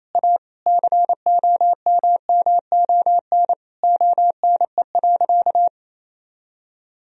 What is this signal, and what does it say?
Morse code, heard as a keyed tone, about 28 wpm, 700 Hz. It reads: A COMMON ONE.
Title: Sentences from Top 1000 Words - Review Part 4 - 28wpm